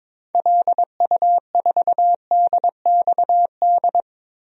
Morse code, heard as a keyed tone, about 22 wpm, 700 Hz. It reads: LU4DXD
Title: DXCC-100 Call Signs - 6 Characters - Repeated 3x - 22wpm